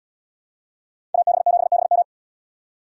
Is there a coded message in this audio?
Yes: Morse code from 1.1 to 2.0 s, heard as a keyed tone, about 74 wpm, 700 Hz. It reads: AB7BM